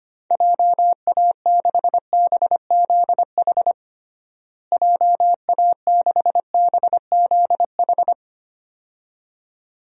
Morse code, heard as a keyed tone, about 25 wpm, 700 Hz. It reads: JA6BZH JA6BZH
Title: DXCC-100 Call Signs - 6 Characters - Repeated 2x - 25wpm